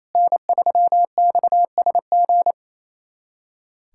Morse code, heard as a keyed tone, about 28 wpm, 700 Hz. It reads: N3XSG
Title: USA Callsign DE Callsign - 5 Characters - 28wpm